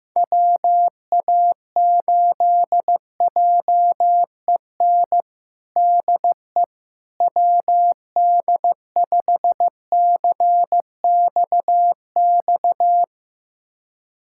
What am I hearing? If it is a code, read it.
Morse code, heard as a keyed tone, about 15 wpm, 700 Hz. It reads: WA8JEN DE WD5CXX
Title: USA Callsign DE Callsign 15wpm